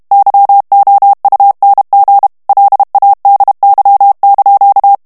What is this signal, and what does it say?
Morse code, heard as a keyed tone, about 32 wpm, 800 Hz. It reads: YOUNG LADY)
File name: ITALIEN texte clair 1 vitesse 30 mots minute